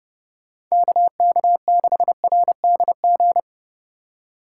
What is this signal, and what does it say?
Morse code, heard as a keyed tone, about 35 wpm, 700 Hz. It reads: KK6RDG
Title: Speed Racing - USA 2x3 Call Signs 30wpm